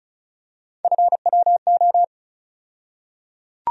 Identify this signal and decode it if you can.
Morse code, heard as a keyed tone, about 36 wpm, 700 Hz. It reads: FWO
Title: USA Call Sign Postfixes - 3 Letters 35wpm